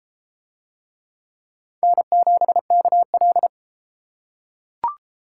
Morse code, heard as a keyed tone, about 31 wpm, 700 Hz. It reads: N7KL